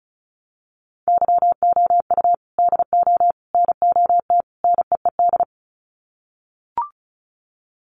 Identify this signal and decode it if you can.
Morse code, heard as a keyed tone, about 34 wpm, 700 Hz. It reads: YOU DO NOT NEED
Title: Sets of 4 Words 35wpm